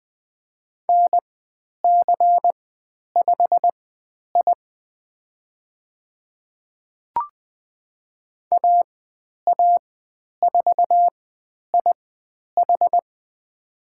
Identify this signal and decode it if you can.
Morse code, heard as a keyed tone, about 20 wpm, spaced as for 10 wpm, 700 Hz. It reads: NC5I AA4IH
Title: 109 Lesson 18g - Letter m - Call Signs 20wpm